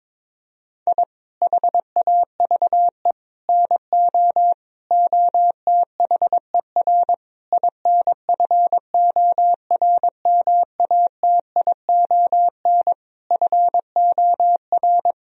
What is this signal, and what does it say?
Morse code, heard as a keyed tone, about 22 wpm, 700 Hz. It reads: I HAVE NO OTHER INFORMATION FOR
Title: Sentences from Top 1000 Words - Review Part 3 - 22wpm